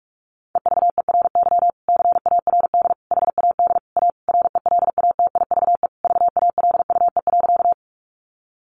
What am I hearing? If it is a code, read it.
Morse code, heard as a keyed tone, about 44 wpm, 700 Hz. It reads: EVERY CARD HAD A RELATIVE VALUE.
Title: Sentences from Top 1400 Words 45wpm